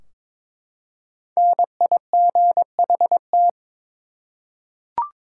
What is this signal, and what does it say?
Morse code, heard as a keyed tone, about 23 wpm, 700 Hz. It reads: NIGHT
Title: Five Letter Words - 22wpm